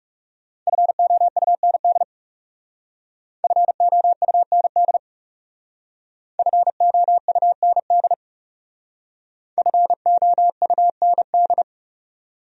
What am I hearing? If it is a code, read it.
Morse code, heard as a keyed tone, about 38 wpm, 700 Hz. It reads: FOUND FOUND FOUND FOUND
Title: Speed-Racing Top 500 Words 30wpm